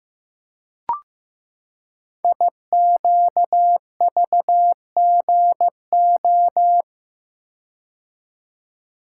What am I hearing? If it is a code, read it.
Morse code, heard as a keyed tone, about 15 wpm, 700 Hz. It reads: IQVGO
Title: Code Groups - 5 Letters - 15wpm